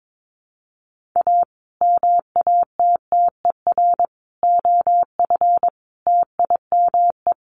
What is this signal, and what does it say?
Morse code, heard as a keyed tone, about 22 wpm, 700 Hz. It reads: A MATTER OF TIME
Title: Sets of 5 Words Encore 22wpm